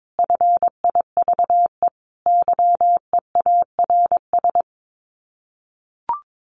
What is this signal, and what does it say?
Morse code, heard as a keyed tone, about 22 wpm, 700 Hz. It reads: FIVE YEARS
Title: Sets of 5 Words Encore 22wpm